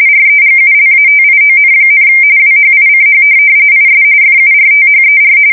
RTTY, about 45.45 baud. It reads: REGULATION, WHICH IS FURTHER
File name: fsk170 / printing-telegraph